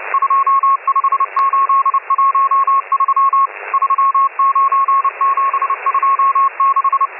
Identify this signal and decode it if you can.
Morse code, heard as a keyed tone, about 30 wpm, 1090 Hz, with radio noise. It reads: 25813 39726